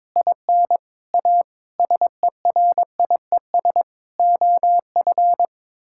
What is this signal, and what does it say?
Morse code, heard as a keyed tone, about 23 wpm, 700 Hz. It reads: IN A SERIES OF
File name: practice / 22wpm / Sets of 6 Words - 22wpm